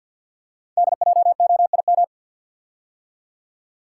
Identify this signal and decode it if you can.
Morse code, heard as a keyed tone, about 50 wpm, 690 Hz. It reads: DJOIM
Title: Code Groups - 5 Letters - 50wpm